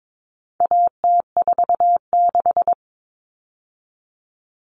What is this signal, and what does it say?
Morse code, heard as a keyed tone, about 22 wpm, 700 Hz. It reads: AT46
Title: All Cut Numbers - 4-Digit 22wpm